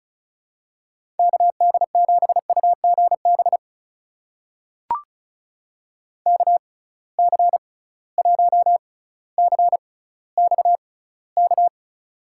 Morse code, heard as a keyed tone, about 36 wpm, spaced as for 18 wpm, 700 Hz. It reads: KD7UGB KC1CXK
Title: USA Call Signs - 6 Characters - 6x Character Spacing - 35wpm